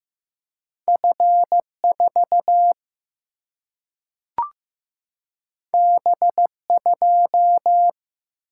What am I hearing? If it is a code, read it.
Morse code, heard as a keyed tone, about 15 wpm, 700 Hz. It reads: F4 B2
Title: Single-Letter Single-Number 15wpm